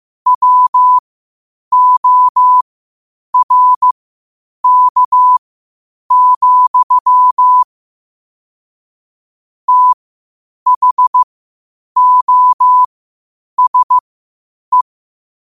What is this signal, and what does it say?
Morse code, heard as a keyed tone, about 15 wpm, spaced as for 8 wpm, 1000 Hz. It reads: WORK, THOSE
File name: qotdfiles / 8wpm